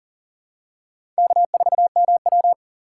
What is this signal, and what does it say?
Morse code, heard as a keyed tone, about 39 wpm, 690 Hz. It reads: K4MW
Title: USA Call Signs - 4 Characters - 2x Character Spacing - 40wpm